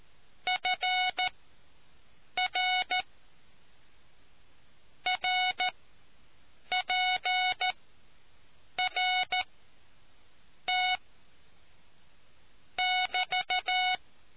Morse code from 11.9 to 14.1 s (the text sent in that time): <BT>